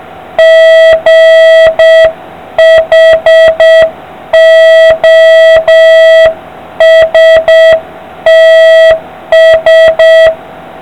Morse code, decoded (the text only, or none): GHOSTS